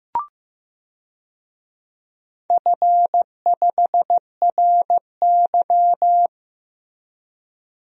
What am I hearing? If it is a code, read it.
Morse code, heard as a keyed tone, about 15 wpm, 700 Hz. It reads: F5RY
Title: DXCC-100 Call Signs - 4 Characters - Repeated 2x - 15wpm